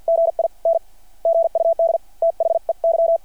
Morse code, heard as a keyed tone, about 51 wpm, spaced as for 42 wpm, 660 Hz. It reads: GIN GUD THEY
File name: archived / CRSnet